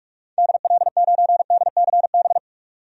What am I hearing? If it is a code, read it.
Morse code, heard as a keyed tone, about 47 wpm, 700 Hz. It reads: DL9DCB
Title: DXCC-100 Call Signs - 6 Characters - 45wpm